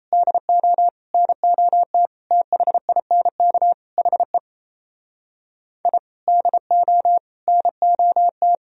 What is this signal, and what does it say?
Morse code, heard as a keyed tone, about 31 wpm, 700 Hz. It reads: DO NOT THINK HE I DO NOT
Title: Speed-Racing - Sets of 5 Words 28wpm